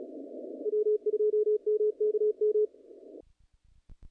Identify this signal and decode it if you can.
Morse code, heard as a keyed tone, about 33 wpm, 420 Hz, with radio noise. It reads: W2MKM
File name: contests / naqp-cw-k5tr-op-k5pi- -0100